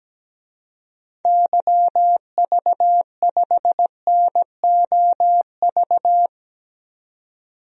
Morse code, heard as a keyed tone, about 17 wpm, 700 Hz. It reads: YV5NOV
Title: DXCC-100 Call Signs - 6 Characters - 17wpm